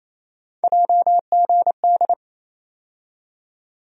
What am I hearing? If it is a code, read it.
Morse code, heard as a keyed tone, about 28 wpm, 700 Hz. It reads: JGD
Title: USA Call Sign Postfixes - 3 Letters 28wpm